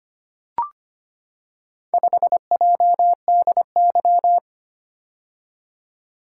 Morse code, heard as a keyed tone, about 25 wpm, 700 Hz. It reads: 5JDY
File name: practice / 25wpm / Four Letter-Number Combinations - 25wpm